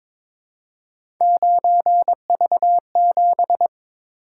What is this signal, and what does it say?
Morse code, heard as a keyed tone, about 22 wpm, 700 Hz. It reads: 9V7
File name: practice / 22wpm / Three Letter-Number Combinations - 22wpm